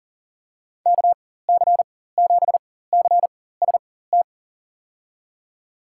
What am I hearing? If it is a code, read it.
Morse code, heard as a keyed tone, about 40 wpm, spaced as for 19 wpm, 700 Hz. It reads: KC7CST
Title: USA Call Signs - 6 Characters - 4x Character Spacing - 40wpm